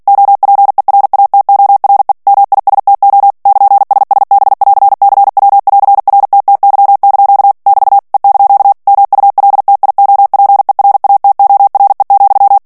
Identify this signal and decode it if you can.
Morse code, heard as a keyed tone, about 48 wpm, 800 Hz. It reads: OPERATORE MISTO <KN>SSB<AR>CW<AR>RTTY) <BT> E) MULTIOPERATORE,
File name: ITALIEN texte clair 2 vitesse 45 mots minute